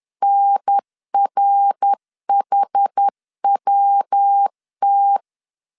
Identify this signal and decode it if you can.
Morse code, heard as a keyed tone, about 10 wpm, 800 Hz. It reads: NRHWT
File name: CW / DET / 1 3 1500C S B 10wpm 3335